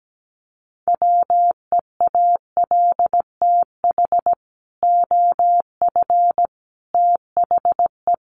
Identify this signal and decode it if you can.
Morse code, heard as a keyed tone, about 17 wpm, 700 Hz. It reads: WEALTH OF THE
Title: Sentences from Top 1200 Words 17wpm